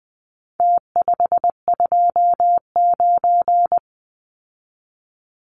Morse code, heard as a keyed tone, about 20 wpm, 700 Hz. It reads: T529
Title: Numbers - Contest Style - 4-Digit 20wpm